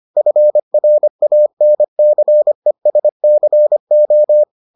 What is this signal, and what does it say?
Morse code, heard as a keyed tone, about 25 wpm, 600 Hz. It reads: FRANCESCO